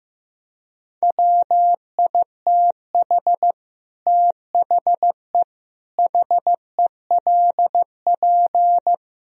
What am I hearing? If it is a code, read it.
Morse code, heard as a keyed tone, about 16 wpm, 700 Hz. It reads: WITH THE HELP
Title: Speed-Racing - Sets of 5 Words 15wpm